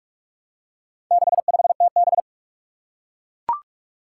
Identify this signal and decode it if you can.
Morse code, heard as a keyed tone, about 44 wpm, 700 Hz. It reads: 65TB
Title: Four Challenging Letter-Number Combinations 45wpm